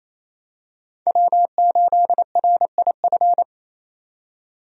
Morse code, heard as a keyed tone, about 27 wpm, 700 Hz. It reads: W8RIF
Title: USA Call Signs - 5 Characters - 28wpm